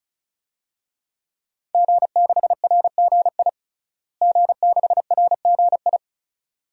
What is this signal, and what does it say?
Morse code, heard as a keyed tone, about 36 wpm, 700 Hz. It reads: G6RGI G6RGI
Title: DXCC-100 Call Signs - 5 Characters - Repeated 2x - 35wpm